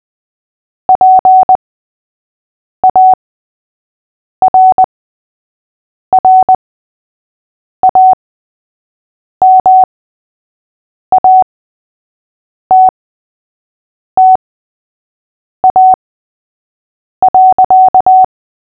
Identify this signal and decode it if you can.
Morse code, heard as a keyed tone, about 20 wpm, spaced as for 6 wpm, 750 Hz. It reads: PARRAMATTA.
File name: cw / fists / Short-Story-3-6wpm